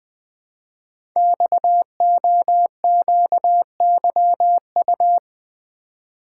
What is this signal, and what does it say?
Morse code, heard as a keyed tone, about 20 wpm, 700 Hz. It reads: XOQYU
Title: Code Groups - 5 Letters - 20wpm